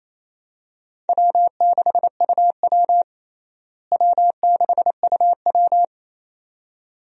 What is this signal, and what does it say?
Morse code, heard as a keyed tone, about 28 wpm, 700 Hz. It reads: W6UW W6UW
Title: USA Call Signs - 4 Characters - Repeated 2x - 28wpm